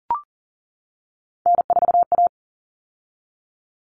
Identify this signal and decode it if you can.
Morse code, heard as a keyed tone, about 40 wpm, 700 Hz. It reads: N4A
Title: All Cut Numbers - 3-Digit 40wpm